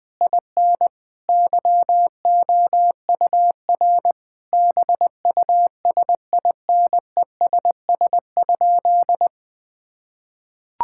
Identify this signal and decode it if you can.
Morse code, heard as a keyed tone, about 20 wpm, 700 Hz. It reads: IN YOUR BUSINESS?
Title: Sentences from Top 1000 Words - Review Part 6 - 20wpm